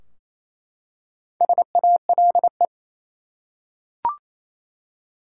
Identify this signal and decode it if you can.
Morse code, heard as a keyed tone, about 28 wpm, 700 Hz. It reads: SALE
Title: Four Letter Words - 28wpm